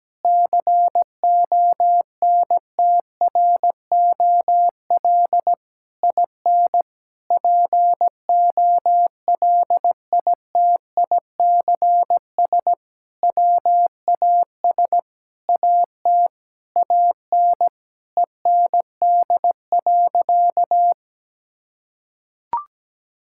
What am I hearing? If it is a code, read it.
Morse code, heard as a keyed tone, about 17 wpm, 700 Hz. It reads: CONTROL IN POLITICS WAS AT AN END.
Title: Sentences from Top 1000 Words - Review Part 6 - 17wpm